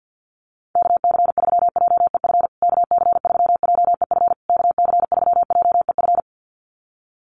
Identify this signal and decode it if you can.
Morse code, heard as a keyed tone, about 50 wpm, 700 Hz. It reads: KC3JEF KC3JEF KC3JEF
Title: USA Call Signs Repeated 3x 50wpm